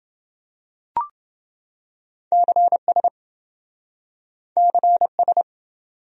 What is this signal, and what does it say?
Morse code, heard as a keyed tone, about 26 wpm, 700 Hz. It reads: CS CS